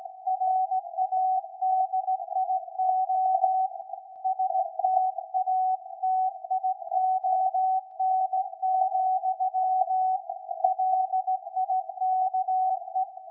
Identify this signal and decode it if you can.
Morse code, heard as a keyed tone, about 17 wpm, 750 Hz. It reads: RADIO STATION, LIKE
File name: iz7slz / VLF / SAQ 13-02-2023